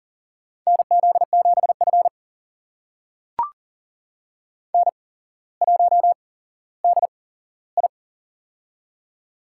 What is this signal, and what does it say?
Morse code, heard as a keyed tone, about 40 wpm, spaced as for 17 wpm, 700 Hz. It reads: NZ7F N1DI
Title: USA Call Signs - 4 Characters - 8x Character Spacing - 40wpm